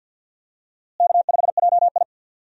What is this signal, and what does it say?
Morse code, heard as a keyed tone, about 49 wpm, 700 Hz. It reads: K5JI
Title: USA Call Signs - 4 Characters - Repeated 3x - 50wpm